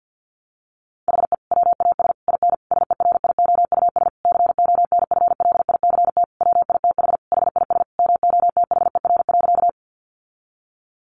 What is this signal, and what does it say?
Morse code, heard as a keyed tone, about 50 wpm, 700 Hz. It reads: HE WAS IN SERIOUS CONFLICT WITH HIS MOTHER.
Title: Sentences from Top 1300 Words 50wpm